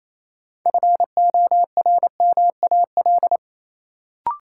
Morse code, heard as a keyed tone, about 28 wpm, 700 Hz. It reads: FORMAL
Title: Sentences from Top 1300 Words - New Word Drill Alt - 28wpm